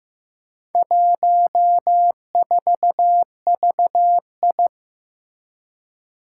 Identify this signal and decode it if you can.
Morse code, heard as a keyed tone, about 15 wpm, 700 Hz. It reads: 14VI
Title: Four Challenging Letter-Number Combinations 15wpm